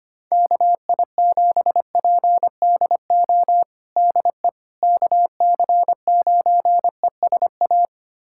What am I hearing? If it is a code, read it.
Morse code, heard as a keyed tone, about 25 wpm, 700 Hz. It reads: KI7PDO DE KC9ESA
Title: USA Callsign DE Callsign - 6 Characters - 25wpm